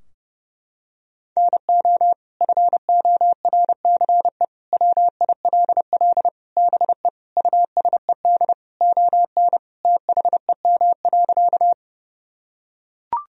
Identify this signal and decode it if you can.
Morse code, heard as a keyed tone, about 30 wpm, 700 Hz. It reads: NO FORCE WILL BE USED ON THEM.
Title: Sentences from Top 1000 Words - Review Part 3 - 30wpm